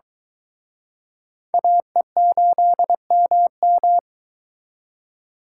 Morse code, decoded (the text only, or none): AE8MM